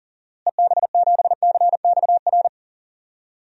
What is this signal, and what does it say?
Morse code, heard as a keyed tone, about 40 wpm, 700 Hz. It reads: EB7CXR
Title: DXCC-100 Call Signs - 6 Characters - Repeated 3x - 40wpm